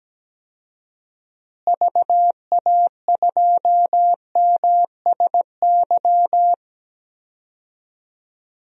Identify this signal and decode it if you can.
Morse code, heard as a keyed tone, about 17 wpm, 700 Hz. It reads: VA2MSY
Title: DXCC-100 Call Signs - 6 Characters - Repeated 3x - 17wpm